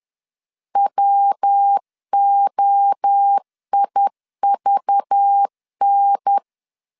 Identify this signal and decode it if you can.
Morse code, heard as a keyed tone, about 10 wpm, 800 Hz. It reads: WOIVN